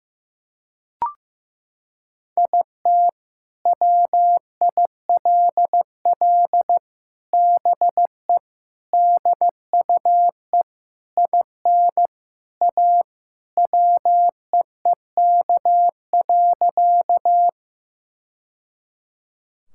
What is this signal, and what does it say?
Morse code, heard as a keyed tone, about 15 wpm, 700 Hz. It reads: IT WILL BE DUE IN A WEEK.
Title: Sentences from Top 1000 Words - Review Part 6 - 15wpm